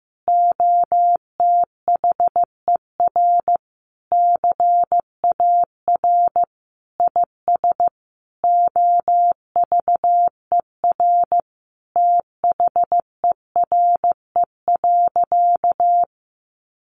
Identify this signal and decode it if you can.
Morse code, heard as a keyed tone, about 15 wpm, 700 Hz. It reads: OTHER CAR IS OVER THERE.
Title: Sentences from Top 1000 Words 15wpm